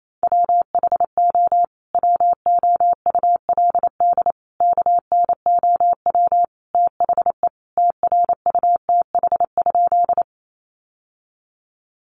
Morse code, heard as a keyed tone, about 28 wpm, 700 Hz. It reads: WHO WOULD KNOW THE TRUTH?
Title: Sentences from Top 800 Words 28wpm